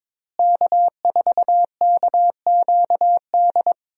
Morse code, heard as a keyed tone, about 22 wpm, 700 Hz. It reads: K4KQD
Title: USA Call Signs - 5 Characters - Repeated 3x - 22wpm